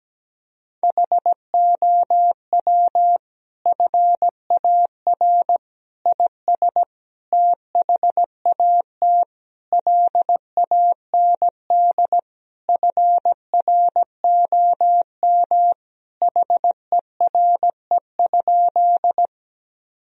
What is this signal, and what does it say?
Morse code, heard as a keyed tone, about 17 wpm, 700 Hz. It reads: HOW FAR IS THAT LAND FROM HERE?
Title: Sentences from Top 1000 Words - Review Part 1 - 17wpm